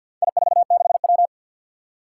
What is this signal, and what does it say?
Morse code, heard as a keyed tone, about 51 wpm, 700 Hz. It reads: I46W
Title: Four Challenging Letter-Number Combinations 50wpm